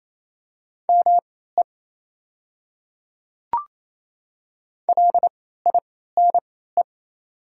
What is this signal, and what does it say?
Morse code, heard as a keyed tone, about 28 wpm, spaced as for 16 wpm, 700 Hz. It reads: ME LINE
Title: 3x Character Spacing - Top 200 Words - 28wpm